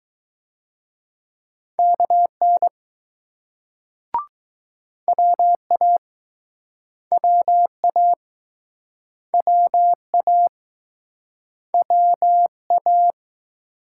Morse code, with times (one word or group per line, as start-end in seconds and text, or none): 1.8-2.7 KN
5.1-6.0 WA
7.1-8.1 WA
9.3-10.5 WA
11.7-13.1 WA